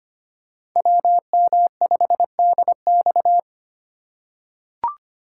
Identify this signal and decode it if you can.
Morse code, heard as a keyed tone, about 25 wpm, 700 Hz. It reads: WM5DX
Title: K1USN SST QSO Practice - 25wpm